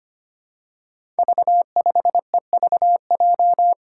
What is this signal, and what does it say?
Morse code, heard as a keyed tone, about 25 wpm, 700 Hz. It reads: V5EVJ
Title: Five Challenging Letter-Number Combinations 25wpm